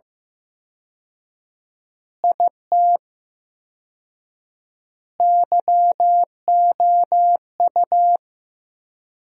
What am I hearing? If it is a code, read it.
Morse code, heard as a keyed tone, about 15 wpm, 700 Hz. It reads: IT YOU
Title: 4x Word-Spacing - Sentences from Top 200 Words - 15wpm